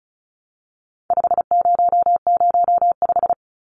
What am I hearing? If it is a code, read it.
Morse code, heard as a keyed tone, about 34 wpm, 700 Hz. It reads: 5005
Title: Numbers - 4-Digit 35wpm